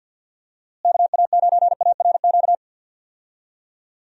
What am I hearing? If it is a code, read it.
Morse code, heard as a keyed tone, about 49 wpm, 700 Hz. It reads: KA9ARQ
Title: USA Call Signs - 6 Characters - Repeated 2x - 50wpm